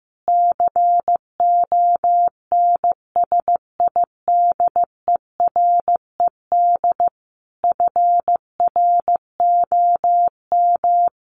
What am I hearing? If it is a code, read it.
Morse code, heard as a keyed tone, about 15 wpm, 700 Hz. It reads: CONSIDERED FROM
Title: Sets of 5 Words Encore 15wpm